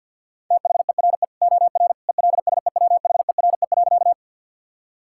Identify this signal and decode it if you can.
Morse code, heard as a keyed tone, about 50 wpm, 700 Hz. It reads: THERE OR ELSEWHERE.
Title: Sentences from Top 1600 Words 50wpm